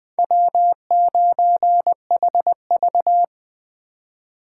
Morse code, heard as a keyed tone, about 20 wpm, 700 Hz. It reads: W9HV
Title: Speed Racing - USA 1x2 Call Signs 17wpm